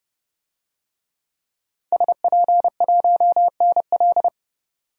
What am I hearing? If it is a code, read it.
Morse code, heard as a keyed tone, about 31 wpm, 700 Hz. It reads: SP1NL
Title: DXCC-100 Call Signs - 5 Characters - 30wpm